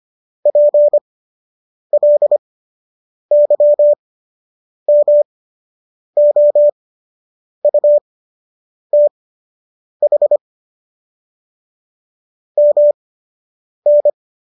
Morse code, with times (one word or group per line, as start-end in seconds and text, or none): 0.5-10.4 PLYMOUTH
12.6-14.1 MN